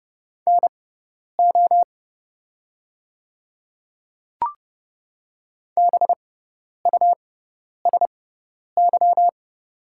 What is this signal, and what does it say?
Morse code, heard as a keyed tone, about 30 wpm, spaced as for 10 wpm, 700 Hz. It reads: NO BUSY
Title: 6x Character Spacing - Top 500 Words - 30wpm